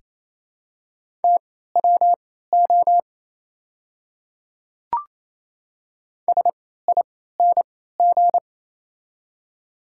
Morse code, heard as a keyed tone, about 27 wpm, spaced as for 16 wpm, 690 Hz. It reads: TWO SING
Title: 3x Character Spacing - Top 500 Words - 28wpm